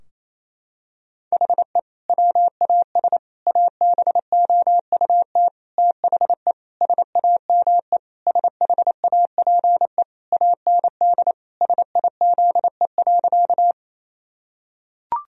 Morse code, heard as a keyed tone, about 27 wpm, 700 Hz. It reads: HE WAS ABOUT THE SAME SHAPE AND SIZE.
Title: Sentences from Top 1000 Words - Review Part 3 - 28wpm